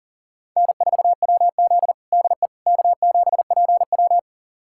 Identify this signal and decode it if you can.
Morse code, heard as a keyed tone, about 40 wpm, 700 Hz. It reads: N4WZ DE K7PW
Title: USA Callsign DE Callsign - 4 Characters - 40wpm